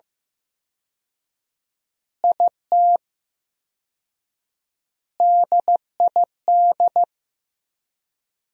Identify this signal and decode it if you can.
Morse code, heard as a keyed tone, about 15 wpm, 700 Hz. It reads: IT DID
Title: 4x Word-Spacing - Sentences from Top 200 Words - 15wpm